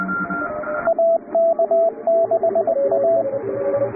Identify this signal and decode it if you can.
Morse code, heard as a keyed tone, about 20 wpm, 670 Hz, with radio noise. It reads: AK6A
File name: chatham / 0216-15CW